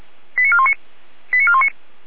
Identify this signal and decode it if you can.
Five-tone selective call (ZVEI), heard as quick runs of five tones, tones of about 70 ms. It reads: 89419 89419